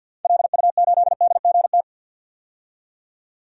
Morse code, heard as a keyed tone, about 52 wpm, 700 Hz. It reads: LU8DGT
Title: DXCC-100 Call Signs - 6 Characters - 50wpm